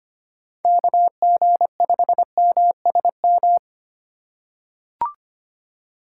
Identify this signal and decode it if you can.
Morse code, heard as a keyed tone, about 25 wpm, 700 Hz. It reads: KG5MSM
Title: USA Call Signs - 6 Characters - 4x Character Spacing - 25wpm